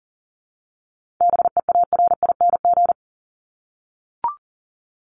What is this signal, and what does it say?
Morse code, heard as a keyed tone, about 40 wpm, 700 Hz. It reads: BEARING
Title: ING Suffix - Top 100 Words with Base Words 40wpm